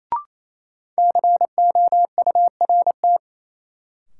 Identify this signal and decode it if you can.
Morse code, heard as a keyed tone, about 28 wpm, 700 Hz. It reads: COURT
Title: Sentences from Top 1600 Words - New Word Drill Alt - 28wpm